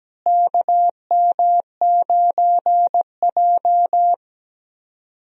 Morse code, heard as a keyed tone, about 17 wpm, 700 Hz. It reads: KM9J
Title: USA Call Signs - 4 Characters - Repeated 3x - 17wpm